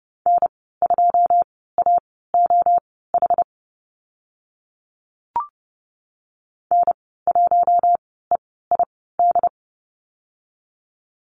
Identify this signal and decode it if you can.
Morse code, heard as a keyed tone, about 30 wpm, spaced as for 17 wpm, 700 Hz. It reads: N2AOH N1EID
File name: Lesson 15g - Number 5 - Call Signs 30wpm